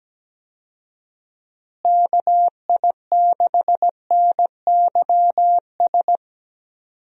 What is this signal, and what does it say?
Morse code, heard as a keyed tone, about 17 wpm, 700 Hz. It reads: KI6NYS
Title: USA Call Signs - 6 Characters - Repeated 3x - 17wpm